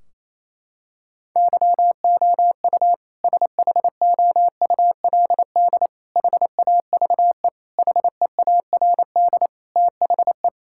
Morse code, heard as a keyed tone, about 28 wpm, 700 Hz. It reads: YOU SHOULD HAVE HEARD THE
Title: Sentences from Top 1600 Words 28wpm